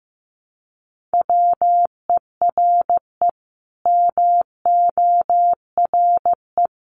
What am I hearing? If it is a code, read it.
Morse code, heard as a keyed tone, about 15 wpm, 700 Hz. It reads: WERE MORE
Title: Sets of 5 Words Encore2 15wpm